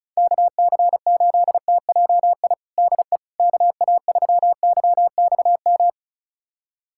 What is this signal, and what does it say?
Morse code, heard as a keyed tone, about 36 wpm, 700 Hz. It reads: KC8TJI DE KA3YXM
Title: USA Callsign DE Callsign - 6 Characters - 35wpm